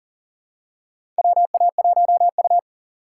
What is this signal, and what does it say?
Morse code, heard as a keyed tone, about 40 wpm, 700 Hz. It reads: WA1U
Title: USA Call Signs - 4 Characters - 6x Character Spacing - 40wpm